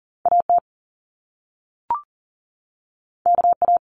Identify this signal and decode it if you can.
Morse code, heard as a keyed tone, about 42 wpm, 700 Hz. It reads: AT KA
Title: Top Two Letter Combinations in Words 40wpm